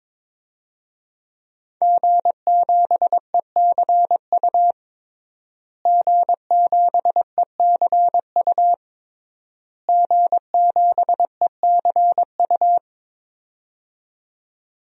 Morse code, heard as a keyed tone, about 22 wpm, 700 Hz. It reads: G7ECU G7ECU G7ECU
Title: DXCC-100 Call Signs - 5 Characters - Repeated 3x - 22wpm